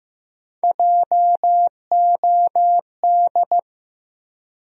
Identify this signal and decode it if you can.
Morse code, heard as a keyed tone, about 15 wpm, 700 Hz. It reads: JOD
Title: USA Call Sign Postfixes - 3 Letters 15wpm